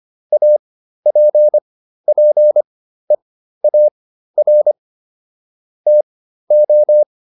Morse code, heard as a keyed tone, about 25 wpm, spaced as for 13 wpm, 600 Hz. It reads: APPEAR TO